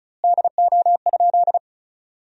Morse code, heard as a keyed tone, about 34 wpm, 700 Hz. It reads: DO?
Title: 2x Word-Spacing - Sentences from Top 200 Words - 35wpm